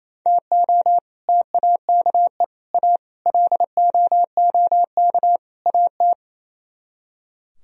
Morse code, heard as a keyed tone, about 36 wpm, 700 Hz. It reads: TO TAKE A LOOK AT